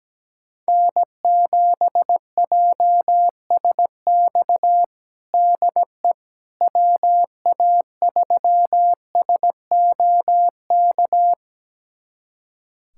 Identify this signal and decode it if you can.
Morse code, heard as a keyed tone, about 17 wpm, 700 Hz. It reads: N7JSX DE WA3SOK
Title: USA Callsign DE Callsign 17wpm